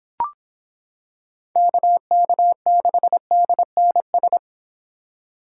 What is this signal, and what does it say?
Morse code, heard as a keyed tone, about 26 wpm, 700 Hz. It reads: KK6DNS